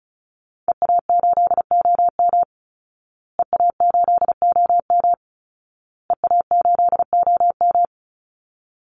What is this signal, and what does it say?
Morse code, heard as a keyed tone, about 34 wpm, 700 Hz. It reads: EA8OM EA8OM EA8OM
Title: Call Signs Repeated 2x - 35wpm